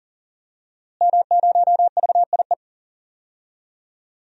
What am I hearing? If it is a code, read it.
Morse code, heard as a keyed tone, about 37 wpm, 700 Hz. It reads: M0VIE
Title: DXCC-100 Call Signs - 5 Characters - Repeated 2x - 40wpm